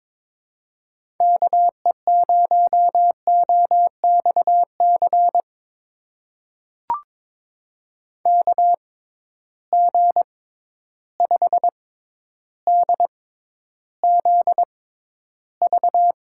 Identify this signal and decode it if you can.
Morse code, heard as a keyed tone, about 22 wpm, spaced as for 11 wpm, 700 Hz. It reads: KE0OXC KG5DZV
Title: USA Call Signs - 6 Characters - 6x Character Spacing - 22wpm